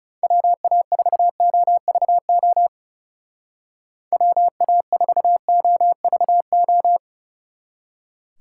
Morse code, heard as a keyed tone, about 36 wpm, 700 Hz. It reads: WA4OVO WA4OVO
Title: Speed Racing - USA 2x3 Call Signs 30wpm